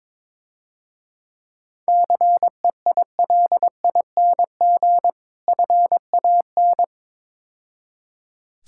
Morse code, heard as a keyed tone, about 22 wpm, 690 Hz. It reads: CEILING FAN
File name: Easy Words - Speed-Racing with Hints - 22wpm